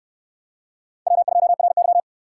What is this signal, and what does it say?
Morse code, heard as a keyed tone, about 66 wpm, 700 Hz. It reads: W2RY